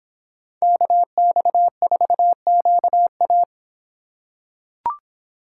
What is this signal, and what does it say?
Morse code, heard as a keyed tone, about 25 wpm, 700 Hz. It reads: KX4QA